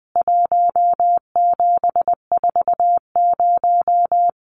Morse code, heard as a keyed tone, about 20 wpm, 700 Hz. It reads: 1740
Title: CWT Exchange 20wpm